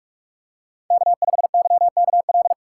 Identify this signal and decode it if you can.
Morse code, heard as a keyed tone, about 46 wpm, 700 Hz. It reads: K5YKL
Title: USA Call Signs - 5 Characters - 4x Character Spacing - 45wpm